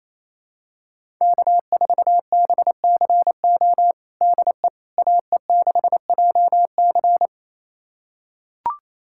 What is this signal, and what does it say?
Morse code, heard as a keyed tone, about 27 wpm, 700 Hz. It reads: K4BCO DE AE6JC
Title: USA Callsign DE Callsign - 5 Characters - 28wpm